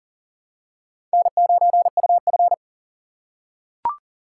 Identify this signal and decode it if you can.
Morse code, heard as a keyed tone, about 40 wpm, 700 Hz. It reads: N9UF
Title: USA Call Signs - 4 Characters - 6x Character Spacing - 40wpm